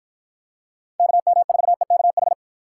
Morse code, heard as a keyed tone, about 53 wpm, 700 Hz. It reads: KM4EKH